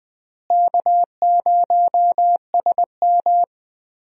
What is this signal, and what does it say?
Morse code, heard as a keyed tone, about 20 wpm, 700 Hz. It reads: K0SM
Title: USA Call Signs - 4 Characters - Repeated 2x - 20wpm